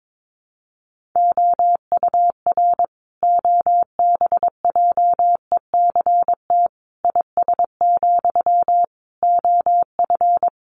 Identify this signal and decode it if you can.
Morse code, heard as a keyed tone, about 22 wpm, 700 Hz. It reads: OUR OBJECT IS, OF
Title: Sentences from Top 1500 Words 22wpm